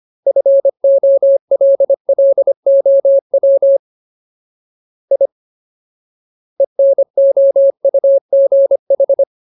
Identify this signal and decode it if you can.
Morse code, heard as a keyed tone, about 25 wpm, 550 Hz. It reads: FOLLOW I ENOUGH
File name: Ultimate Word List / 25WPM Ultimate